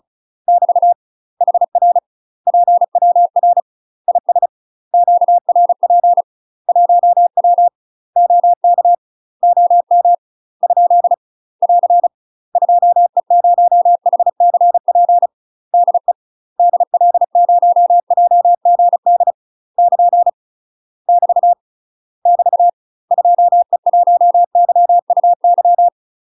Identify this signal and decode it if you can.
Morse code, heard as a keyed tone, about 34 wpm, 700 Hz. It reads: <BT> HR PWR IS QRP 1W OK OM ? <AR> 2E0HCP DE DL0JGD <KN> <BT> <BT> 2E1YUY
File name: UK MASTER QSOs TRAINING / 35WPM / UK MASTER QSOs 0008